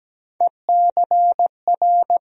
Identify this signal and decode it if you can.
Morse code, heard as a keyed tone, about 17 wpm, 700 Hz. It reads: ECR
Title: USA Call Sign Postfixes - 3 Letters 17wpm